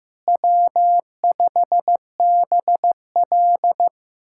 Morse code, heard as a keyed tone, about 16 wpm, 700 Hz. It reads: W5BL
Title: USA Callsign DE Callsign - 4 Characters - 15wpm